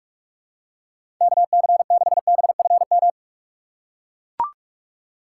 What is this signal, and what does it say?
Morse code, heard as a keyed tone, about 44 wpm, 700 Hz. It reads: KC6BFM
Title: USA Call Signs - 6 Characters - 6x Character Spacing - 45wpm